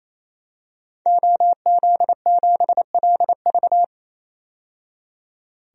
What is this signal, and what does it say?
Morse code, heard as a keyed tone, about 28 wpm, 700 Hz. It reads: OZ7LV
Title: DXCC-100 Call Signs - 5 Characters - 28wpm